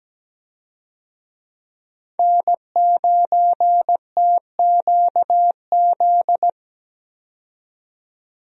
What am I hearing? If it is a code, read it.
Morse code, heard as a keyed tone, about 17 wpm, 700 Hz. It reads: N9TQZ
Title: USA Call Signs - 5 Characters - 17wpm